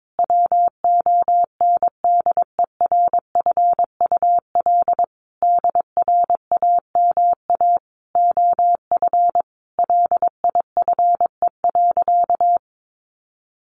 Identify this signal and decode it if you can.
Morse code, heard as a keyed tone, about 22 wpm, 700 Hz. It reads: WONDERFUL DRAMA OF LIFE.
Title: Sentences from Top 1500 Words 22wpm